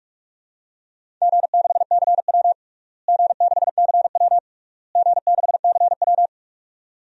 Morse code, heard as a keyed tone, about 46 wpm, 700 Hz. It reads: G6CW G6CW G6CW
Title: DXCC-100 Call Signs - 4 Characters - Repeated 3x - 45wpm